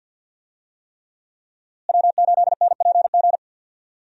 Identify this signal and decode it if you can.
Morse code, heard as a keyed tone, about 50 wpm, 700 Hz. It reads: W8NPG
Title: USA Call Signs - 5 Characters - Repeated 3x - 50wpm